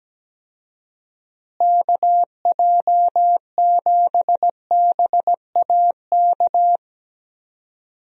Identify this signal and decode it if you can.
Morse code, heard as a keyed tone, about 17 wpm, 700 Hz. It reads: KJ7BAK
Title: USA Call Signs - 6 Characters - Repeated 2x - 17wpm